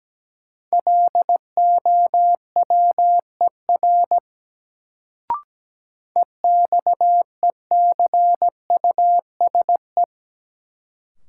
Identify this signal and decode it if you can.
Morse code, heard as a keyed tone, about 17 wpm, 700 Hz. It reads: LOWER EXECUSE
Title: Sentences from Top 700 Words - New Word Drill - 17wpm